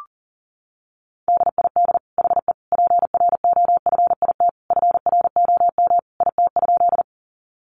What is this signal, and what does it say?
Morse code, heard as a keyed tone, about 40 wpm, 700 Hz. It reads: DID HE PROFIT FROM IT?
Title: Sentences from Top 800 Words 40wpm